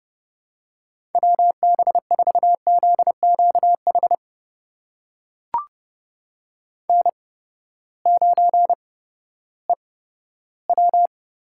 Morse code, heard as a keyed tone, about 30 wpm, spaced as for 15 wpm, 700 Hz. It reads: WB4ZQH N9EW
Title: USA Call Signs - 8x Character Spacing - 30wpm